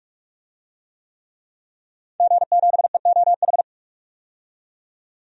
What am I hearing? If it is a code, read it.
Morse code, heard as a keyed tone, about 46 wpm, 700 Hz. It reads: G7EOH
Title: DXCC-100 Call Signs - 5 Characters - 45wpm